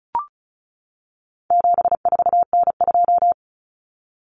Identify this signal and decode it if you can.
Morse code, heard as a keyed tone, about 34 wpm, 700 Hz. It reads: 74N2
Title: Common Cut Numbers - 4-Digit 35wpm